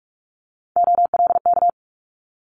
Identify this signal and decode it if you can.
Morse code, heard as a keyed tone, about 46 wpm, 700 Hz. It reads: OLK
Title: Multi-letter Phonemes Quick Fire 45wpm